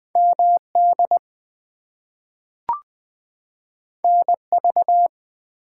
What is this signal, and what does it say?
Morse code, heard as a keyed tone, about 20 wpm, 700 Hz. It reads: MD NV
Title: US States and Canadian Provinces 20wpm